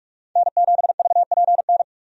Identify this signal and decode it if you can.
Morse code, heard as a keyed tone, about 47 wpm, 700 Hz. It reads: N7VPN
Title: USA Call Signs - 5 Characters - 45wpm